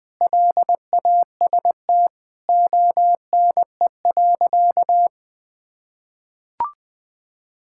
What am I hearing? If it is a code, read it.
Morse code, heard as a keyed tone, about 21 wpm, 700 Hz. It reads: LAST ONE.
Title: Sentences with 3-Word Phrases - 20wpm